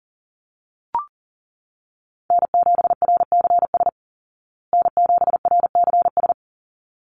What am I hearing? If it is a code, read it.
Morse code, heard as a keyed tone, about 40 wpm, 700 Hz. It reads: N7RCS N7RCS
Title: Call Signs Repeated 3x - 40wpm